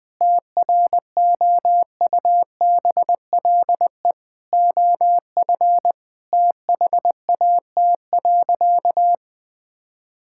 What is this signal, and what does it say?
Morse code, heard as a keyed tone, about 20 wpm, 700 Hz. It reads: TROUBLE OF THAT.
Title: Sentences from Top 1600 Words 20wpm